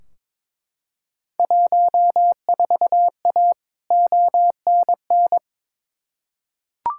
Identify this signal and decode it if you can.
Morse code, heard as a keyed tone, about 22 wpm, 700 Hz. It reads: 14A ONN
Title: Field Day Exchanges Encore - 22wpm